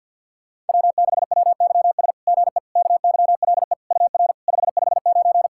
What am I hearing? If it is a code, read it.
Morse code, heard as a keyed tone, about 50 wpm, 690 Hz. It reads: W6WYS GE KYLE UR 559
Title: SOTA QSO Practice 50wpm